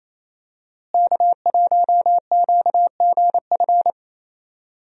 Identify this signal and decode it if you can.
Morse code, heard as a keyed tone, about 27 wpm, 700 Hz. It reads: K1QGF
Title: USA Call Signs - 5 Characters - Repeated 3x - 28wpm